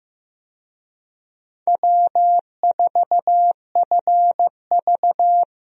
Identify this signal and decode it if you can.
Morse code, heard as a keyed tone, about 16 wpm, 700 Hz. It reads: W4FV
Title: USA Call Signs - 4 Characters - Repeated 2x - 15wpm